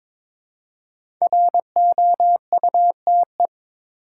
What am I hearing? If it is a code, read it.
Morse code, heard as a keyed tone, about 22 wpm, 700 Hz. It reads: ROUTE
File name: 5 Letter Words - First 3 Letters Given - 22wpm